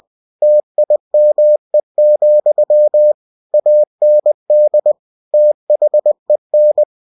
Morse code, heard as a keyed tone, about 20 wpm, 600 Hz. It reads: TIME, AND THEN